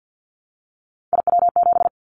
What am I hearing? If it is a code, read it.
Morse code, heard as a keyed tone, about 50 wpm, 700 Hz. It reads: IW7
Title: Callsign Prefixes for Top 25 DXCC - 50wpm